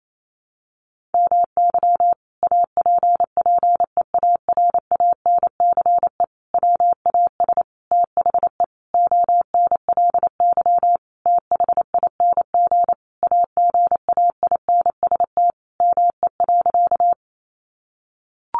Morse code, heard as a keyed tone, about 28 wpm, 700 Hz. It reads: MY APPEARANCE WAS THE ONLY THING AGAINST ME.
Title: Sentences from Top 1100 Words 28wpm